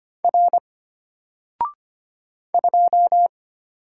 Morse code, heard as a keyed tone, about 25 wpm, 700 Hz. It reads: R 2
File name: Single-Letter-Number-25wpm